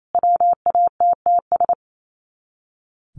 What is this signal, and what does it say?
Morse code, heard as a keyed tone, about 28 wpm, 700 Hz. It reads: WATTS
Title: Common QSO Phrases 28wpm